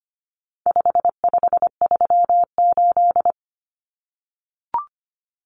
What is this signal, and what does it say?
Morse code, heard as a keyed tone, about 25 wpm, 700 Hz. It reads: 5538
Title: Common Cut Numbers - 4-Digit 25wpm